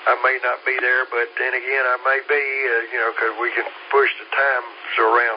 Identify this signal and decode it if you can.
Emergency Alert System (SAME) tones, heard as harsh buzzing bursts, with speech. It is heard from 0.6 to 4.6 s.